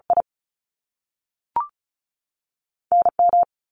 Morse code, heard as a keyed tone, about 36 wpm, spaced as for 3 wpm, 700 Hz. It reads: IN M
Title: US States 35wpm